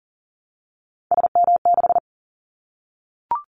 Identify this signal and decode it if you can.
Morse code, heard as a keyed tone, about 40 wpm, 700 Hz. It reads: SM6
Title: Callsign Prefixes for Top 25 DXCC - Repeated 3x - 40wpm